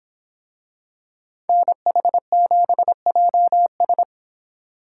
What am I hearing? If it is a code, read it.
Morse code, heard as a keyed tone, about 24 wpm, 700 Hz. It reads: NH7JS